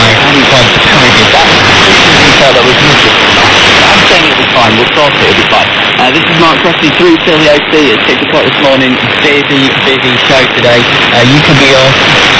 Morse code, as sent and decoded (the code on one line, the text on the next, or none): none